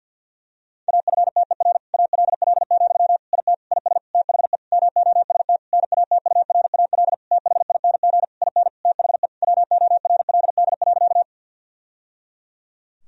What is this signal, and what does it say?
Morse code, heard as a keyed tone, about 50 wpm, 700 Hz. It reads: AFTER ALL, IT IS THE MOST NATURAL THING IN THE WORLD.
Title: Sentences from Top 1000 Words - Review Part 4 - 50wpm